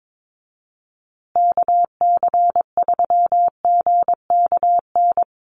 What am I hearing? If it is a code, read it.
Morse code, heard as a keyed tone, about 22 wpm, 700 Hz. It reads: KC3GKN
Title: USA Call Signs Repeated 2x 22wpm